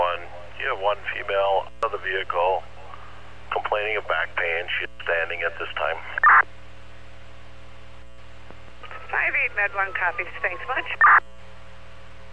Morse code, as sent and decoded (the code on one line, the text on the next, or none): none